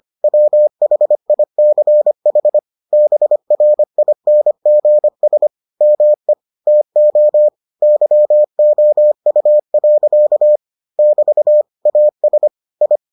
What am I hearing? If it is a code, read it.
Morse code, heard as a keyed tone, about 25 wpm, 600 Hz. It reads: WHICH BRINGS ME TO YOU. <BT> AS I